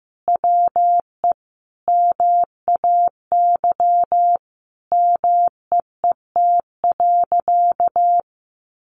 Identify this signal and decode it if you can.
Morse code, heard as a keyed tone, about 15 wpm, 700 Hz. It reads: WE MAY MEET.
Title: Sentences from Top 1100 Words 15wpm